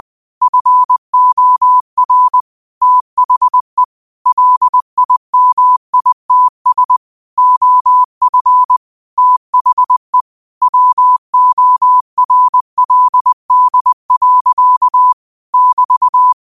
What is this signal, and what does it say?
Morse code, heard as a keyed tone, about 20 wpm, 1000 Hz. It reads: FOR THE LIMITS OF THE WORLD. <BT>
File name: qotdfiles / 20wpm